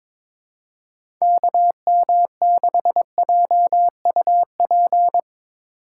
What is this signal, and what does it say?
Morse code, heard as a keyed tone, about 23 wpm, 700 Hz. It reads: KM6JUP
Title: USA Call Signs - 6 Characters - 2x Character Spacing - 22wpm